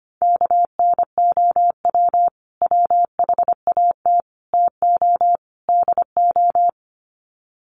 Morse code, heard as a keyed tone, about 25 wpm, 700 Hz. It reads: KNOW WHAT TO DO
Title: Sets of 4 Words 25wpm